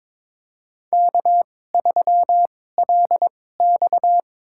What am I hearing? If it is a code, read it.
Morse code, heard as a keyed tone, about 22 wpm, spaced as for 16 wpm, 700 Hz. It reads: K3LX
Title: USA Call Signs - 4 Characters - 2x Character Spacing - 22wpm